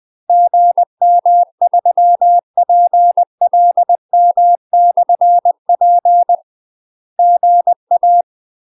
Morse code, heard as a keyed tone, about 20 wpm, 700 Hz. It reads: GM3PLM/P GA
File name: UK MASTER QSOs TRAINING / 20WPM / UK MASTER QSOs 0049